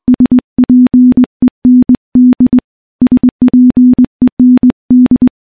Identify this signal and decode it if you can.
Morse code, heard as a keyed tone, about 21 wpm, 250 Hz. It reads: SPEND SPEND
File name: morse / morse-training-20WPM